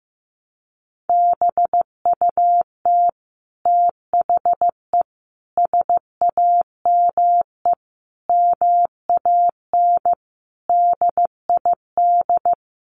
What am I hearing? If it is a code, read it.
Morse code, heard as a keyed tone, about 15 wpm, 700 Hz. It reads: BUT THE SAME MAN DID
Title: A Sound Of Thunder 15wpm